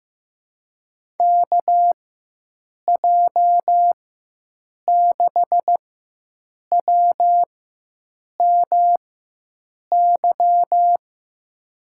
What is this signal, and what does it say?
Morse code, heard as a keyed tone, about 15 wpm, spaced as for 7 wpm, 700 Hz. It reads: KJ6WMY
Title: USA Call Signs - 6 Characters - 4x Character Spacing - 15wpm